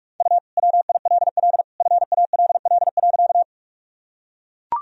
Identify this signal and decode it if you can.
Morse code, heard as a keyed tone, about 42 wpm, 700 Hz. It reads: U WILL FALL.
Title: Sentences from Top 500 Words and Abbreviations 45wpm